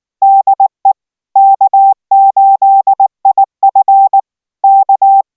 Morse code, heard as a keyed tone, about 19 wpm, 780 Hz. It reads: DE K8IF K